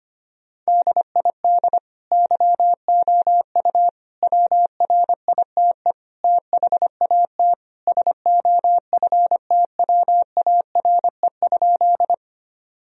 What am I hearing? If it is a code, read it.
Morse code, heard as a keyed tone, about 25 wpm, 700 Hz. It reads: DID YOU WRITE THAT SOFTWARE?
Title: Sentences from Top 1000 Words - Review Part 3 - 25wpm